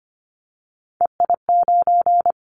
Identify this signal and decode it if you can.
Morse code, heard as a keyed tone, about 25 wpm, 690 Hz. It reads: EI9
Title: Callsign Prefixes for Top 50 DXCC - 25wpm